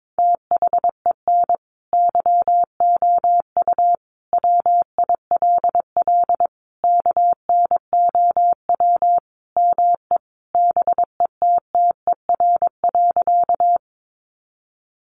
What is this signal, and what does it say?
Morse code, heard as a keyed tone, about 22 wpm, 700 Hz. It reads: THEN YOU WILL KNOW ME BETTER.
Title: Sentences from Top 400 Words 22wpm